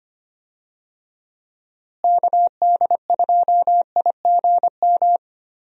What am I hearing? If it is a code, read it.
Morse code, heard as a keyed tone, about 25 wpm, 700 Hz. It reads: KD2IGM
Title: USA Call Signs - 6 Characters - Repeated 2x - 25wpm